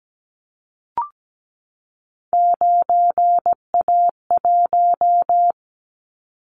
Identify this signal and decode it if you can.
Morse code, heard as a keyed tone, about 17 wpm, 700 Hz. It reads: 9A1
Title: Callsign Prefixes for Top 50 DXCC - 17wpm